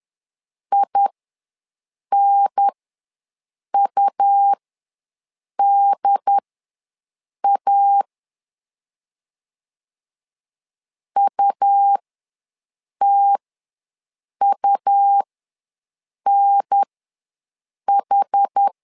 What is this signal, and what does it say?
Morse code, heard as a keyed tone, about 10 wpm, spaced as for 6 wpm, 800 Hz. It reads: INUDA UTUNH
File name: LEZIONI / 2 UHDN / 2 UHDN ws 10wpm 2044